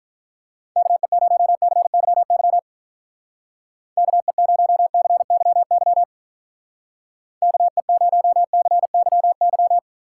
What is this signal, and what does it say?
Morse code, heard as a keyed tone, about 44 wpm, 700 Hz. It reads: KE0CYY KE0CYY KE0CYY